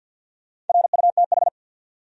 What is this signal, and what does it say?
Morse code, heard as a keyed tone, about 48 wpm, 700 Hz. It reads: RUTH
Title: USA Common Names - Max 5 Letters - 50wpm